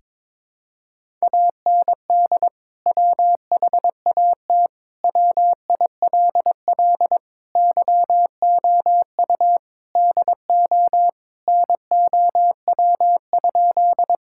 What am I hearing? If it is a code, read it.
Morse code, heard as a keyed tone, about 22 wpm, 700 Hz. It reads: AND WHAT WILL YOU DO NOW?
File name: practice / 22wpm / Sentences with 2-Word Phrases - 22wpm